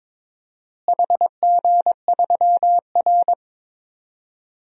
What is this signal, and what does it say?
Morse code, heard as a keyed tone, about 22 wpm, 700 Hz. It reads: HG3R
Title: DXCC-100 Call Signs - 4 Characters - Repeated 2x - 22wpm